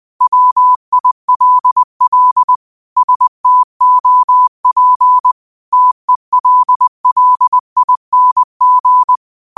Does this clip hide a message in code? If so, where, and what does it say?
Yes: Morse code from 0.2 to 9.2 s, heard as a keyed tone, about 20 wpm, 1000 Hz. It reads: WILL STOP TELLING